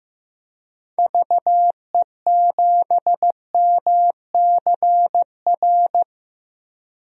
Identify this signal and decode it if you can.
Morse code, heard as a keyed tone, about 15 wpm, 700 Hz. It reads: VE7MCR
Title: DXCC-100 Call Signs - 6 Characters - Repeated 3x - 15wpm